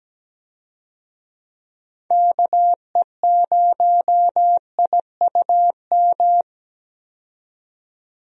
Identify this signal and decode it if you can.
Morse code, heard as a keyed tone, about 17 wpm, 700 Hz. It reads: KE0IUM
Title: USA Call Signs - 6 Characters - 17wpm